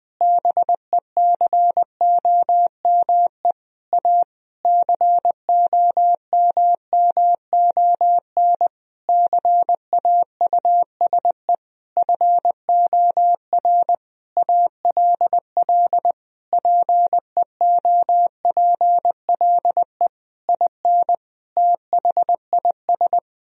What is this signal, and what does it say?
Morse code, heard as a keyed tone, about 20 wpm, 700 Hz. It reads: BECOME A COMMON CAUSE FOR ALL PEOPLE IN THIS
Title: Sentences from Top 1000 Words - Review Part 3 - 20wpm